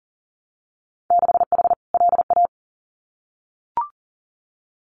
Morse code, heard as a keyed tone, about 40 wpm, 700 Hz. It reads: 6H LA
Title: Winter Field Day Exchanges Encore 40wpm